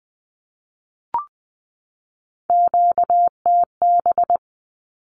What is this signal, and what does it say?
Morse code, heard as a keyed tone, about 20 wpm, 700 Hz. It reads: QTB
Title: Q-Codes 20wpm